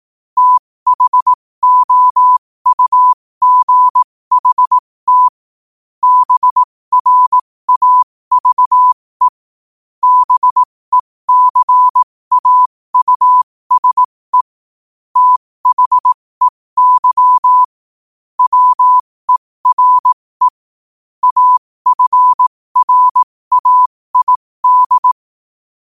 Morse code, heard as a keyed tone, about 18 wpm, spaced as for 15 wpm, 1000 Hz. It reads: THOUGHT BRAVE BECAUSE THEY WERE AFRAID
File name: qotdfiles / 15wpm